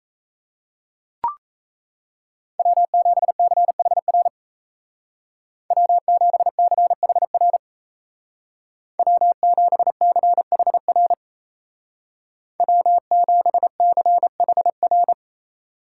Morse code, heard as a keyed tone, about 37 wpm, 700 Hz. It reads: W7CHR W7CHR W7CHR W7CHR